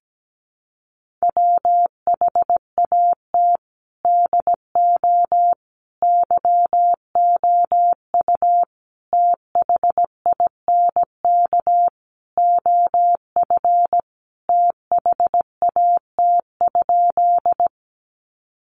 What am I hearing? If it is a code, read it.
Morse code, heard as a keyed tone, about 17 wpm, 700 Hz. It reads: WHAT DO YOU THINK OF THAT?
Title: Sentences from Top 200 Words 17wpm